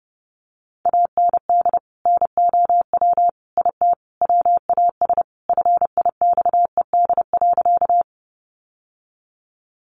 Morse code, heard as a keyed tone, about 30 wpm, 700 Hz. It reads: AND NOW IT WAS FIXED.
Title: Sentences from Top 1100 Words 30wpm